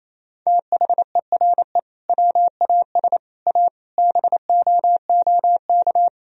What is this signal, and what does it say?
Morse code, heard as a keyed tone, about 28 wpm, 700 Hz. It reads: THERE WAS A BOOK
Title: Sentences from Top 1000 Words - Review Part 3 - 28wpm